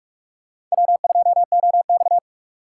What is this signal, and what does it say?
Morse code, heard as a keyed tone, about 46 wpm, 700 Hz. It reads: W2OX
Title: USA Call Signs - 4 Characters - 6x Character Spacing - 45wpm